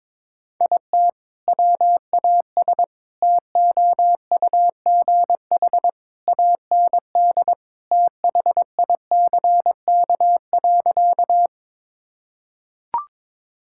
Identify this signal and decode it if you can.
Morse code, heard as a keyed tone, about 22 wpm, 700 Hz. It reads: IT WAS TOUGH AND THICK.
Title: Sentences from Top 1600 Words 22wpm